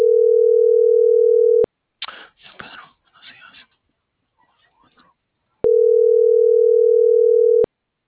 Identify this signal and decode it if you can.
Ringback tone (North America), 440 + 480 Hz.